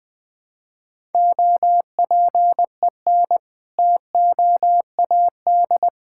Morse code, heard as a keyed tone, about 20 wpm, 700 Hz. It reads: OPEN TOAD
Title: All-Age Appropriate Jokes - 20wpm